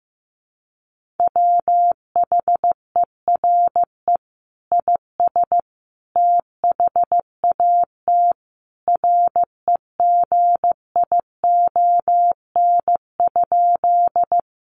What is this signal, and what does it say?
Morse code, heard as a keyed tone, about 14 wpm, 700 Hz. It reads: WHERE IS THAT REGION?
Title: Sentences from Top 1500 Words 15wpm